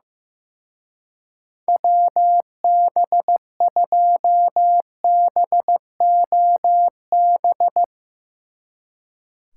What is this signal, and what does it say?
Morse code, heard as a keyed tone, about 15 wpm, 700 Hz. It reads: WB2BOB
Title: Speed Racing - USA 2x3 Call Signs 15wpm